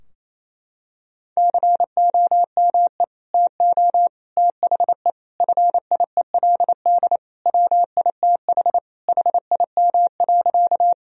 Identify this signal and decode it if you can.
Morse code, heard as a keyed tone, about 28 wpm, 700 Hz. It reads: COME TO THE FIELD WITH HIM.
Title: Sentences from Top 1000 Words - Review Part 2 - 28wpm